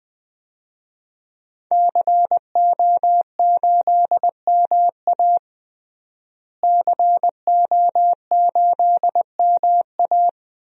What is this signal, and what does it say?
Morse code, heard as a keyed tone, about 20 wpm, 700 Hz. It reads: CO8MA CO8MA
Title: DXCC-100 Call Signs - 5 Characters - Repeated 2x - 20wpm